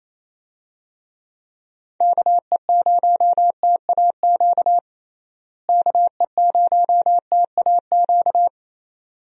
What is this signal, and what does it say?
Morse code, heard as a keyed tone, about 27 wpm, 700 Hz. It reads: KE0TAQ KE0TAQ
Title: USA Call Signs - 6 Characters - Repeated 2x - 28wpm